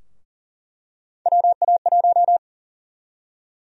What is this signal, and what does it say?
Morse code, heard as a keyed tone, about 40 wpm, 700 Hz. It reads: WA1
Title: USA Call Sign Prefixes - 40wpm